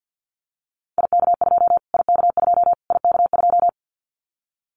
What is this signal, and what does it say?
Morse code, heard as a keyed tone, about 50 wpm, 700 Hz. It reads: IK2 IK2 IK2
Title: Callsign Prefixes for Top 25 DXCC - Repeated 3x - 50wpm